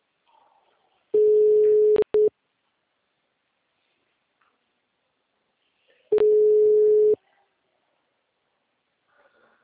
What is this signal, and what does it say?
Ringback tone (Europe, 425 Hz).
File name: out- -2014- - - .24